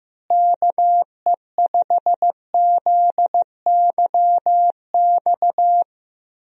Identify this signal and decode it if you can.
Morse code, heard as a keyed tone, about 15 wpm, 700 Hz. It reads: KE5ZYX
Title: USA Call Signs - 6 Characters - Repeated 2x - 15wpm